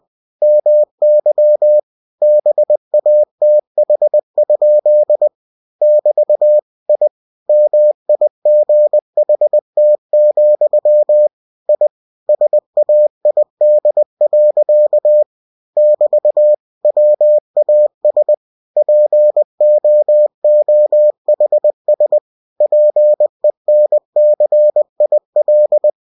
Morse code, heard as a keyed tone, about 20 wpm, 600 Hz. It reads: MY BATH? <BT> I MIGHT, I SAID. <BT> WAS POOHS PENCIL
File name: cw-books / winnie-the-pooh / winnie-20wpm-page-0111